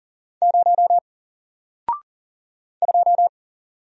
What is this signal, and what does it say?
Morse code, heard as a keyed tone, about 39 wpm, 700 Hz. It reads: 0 2
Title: Single Letter-Number 40wpm